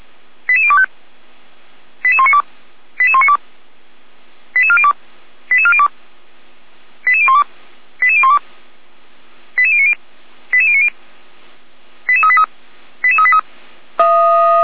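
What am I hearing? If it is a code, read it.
Five-tone selective call (ZVEI), heard as quick runs of five tones, tones of about 70 ms. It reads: 88025 88192 88192 88402 88402 88012 88012 88090 88090 88383 88383
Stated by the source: ZVEI five-tone call 88102 received by SDR, from the Feuerwehr Orlamünde fire brigade audio server.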